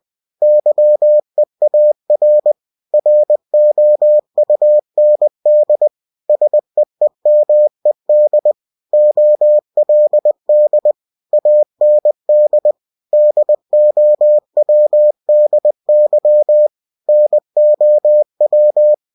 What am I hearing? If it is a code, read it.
Morse code, heard as a keyed tone, about 20 wpm, 600 Hz. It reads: YEAR ROUND SEEMED OLD AND DOWDY NOW
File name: cw-books / winnie-the-pooh / winnie-20wpm-page-0028